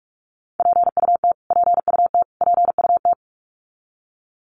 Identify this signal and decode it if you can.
Morse code, heard as a keyed tone, about 46 wpm, 700 Hz. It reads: PUT PUT PUT
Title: Top 500 Words Repeated 45wpm